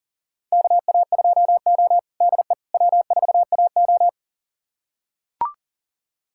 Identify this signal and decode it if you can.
Morse code, heard as a keyed tone, about 40 wpm, 700 Hz. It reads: KA2O DE W4AO
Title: USA Callsign DE Callsign - 4 Characters - 40wpm